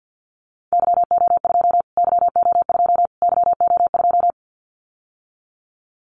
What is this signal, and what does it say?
Morse code, heard as a keyed tone, about 50 wpm, 700 Hz. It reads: YO2 YO2 YO2
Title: Callsign Prefixes for Top 25 DXCC - Repeated 3x - 50wpm